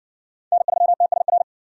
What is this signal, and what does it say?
Morse code, heard as a keyed tone, about 60 wpm, 700 Hz. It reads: N3TSR